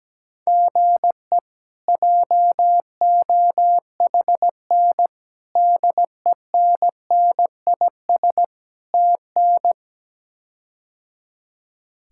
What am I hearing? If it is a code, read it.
Morse code, heard as a keyed tone, about 17 wpm, 700 Hz. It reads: GE JOHN DENNIS TN
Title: K1USN SST QSO Practice - 17wpm